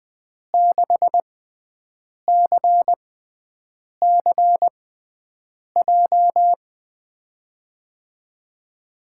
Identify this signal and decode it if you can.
Morse code, heard as a keyed tone, about 21 wpm, spaced as for 7 wpm, 700 Hz. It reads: 6CCJ
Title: USA Call Signs - 6 Characters - 6x Character Spacing - 20wpm